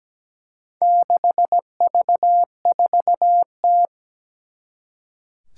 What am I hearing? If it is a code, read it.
Morse code, heard as a keyed tone, about 17 wpm, 700 Hz. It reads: 6V4T
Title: Four Challenging Letter-Number Combinations 17wpm